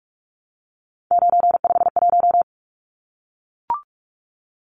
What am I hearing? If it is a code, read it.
Morse code, heard as a keyed tone, about 44 wpm, 700 Hz. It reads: 951
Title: Numbers - 3-Digit 45wpm